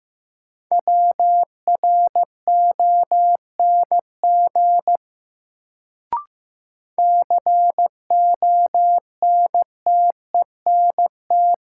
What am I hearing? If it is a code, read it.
Morse code, heard as a keyed tone, about 15 wpm, 700 Hz. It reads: WRONG CONTENT
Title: Sentences from Top 900 Words - New Word Drill - 15wpm